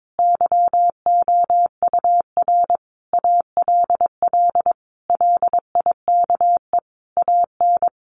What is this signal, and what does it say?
Morse code, heard as a keyed tone, about 22 wpm, 700 Hz. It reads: YOUR ALL LIKE AN
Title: Sentences from Top 1500 Words 22wpm